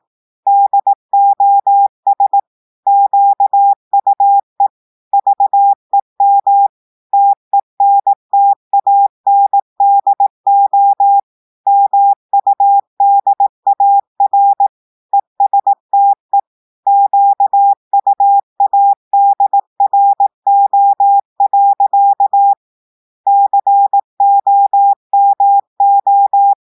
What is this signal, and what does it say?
Morse code, heard as a keyed tone, about 18 wpm, 800 Hz. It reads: DOS QUE VEM TENTANDO MUDAR ESTE QUADRO. COMO
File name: audios / portugais / portugais texte en clair 1 vitesse 18 mots minutes